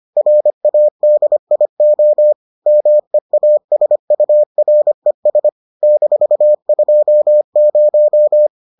Morse code, heard as a keyed tone, about 25 wpm, 600 Hz. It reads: RADIO MEASURES -20